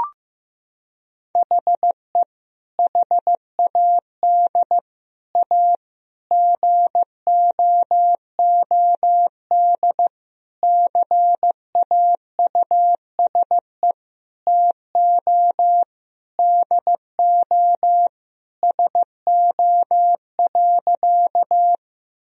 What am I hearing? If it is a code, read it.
Morse code, heard as a keyed tone, about 15 wpm, 700 Hz. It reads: HE HAD A GOOD CAUSE TO DO SO.
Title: Sentences from Top 1000 Words - Review Part 1 - 15wpm